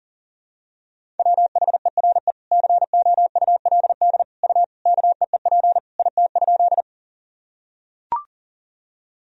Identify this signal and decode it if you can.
Morse code, heard as a keyed tone, about 40 wpm, 700 Hz. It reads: WHERE COULD U KEEP IT?
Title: Sentences from Top 300 Words and Abbreviations 40wpm